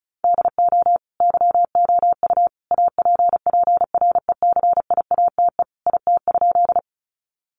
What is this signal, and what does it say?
Morse code, heard as a keyed tone, about 34 wpm, 700 Hz. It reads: DO YOU APPRECIATE IT?
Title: Sentences from Top 1300 Words 35wpm